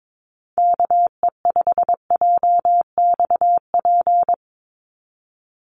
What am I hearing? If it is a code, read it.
Morse code, heard as a keyed tone, about 22 wpm, 700 Hz. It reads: KE5JXP
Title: USA Call Signs 22wpm